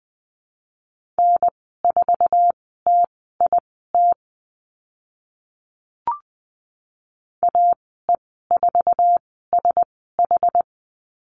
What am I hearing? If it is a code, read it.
Morse code, heard as a keyed tone, about 20 wpm, spaced as for 14 wpm, 700 Hz. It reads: N4TIT AE4SH
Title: 069 Lesson 12g - Letter d - Call Signs 20wpm